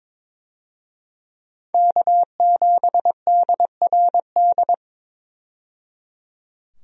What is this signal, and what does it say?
Morse code, heard as a keyed tone, about 22 wpm, 700 Hz. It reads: K7DRD
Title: USA Call Signs - 5 Characters - 22wpm